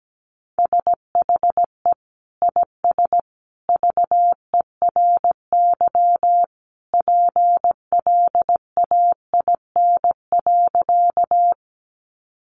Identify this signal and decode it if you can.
Morse code, heard as a keyed tone, about 17 wpm, 700 Hz. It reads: SHE IS VERY PLAIN.
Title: Sentences from Top 400 Words 17wpm